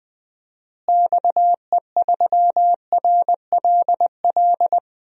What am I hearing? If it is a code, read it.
Morse code, heard as a keyed tone, about 21 wpm, 700 Hz. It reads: XE3RLL
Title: DXCC-100 Call Signs - 6 Characters - Repeated 3x - 20wpm